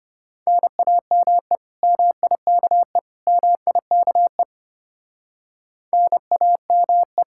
Morse code, heard as a keyed tone, about 31 wpm, 700 Hz. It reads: NAME MIKE MIKE NAME
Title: Speed-Racing Common QSO Phrases - 25wpm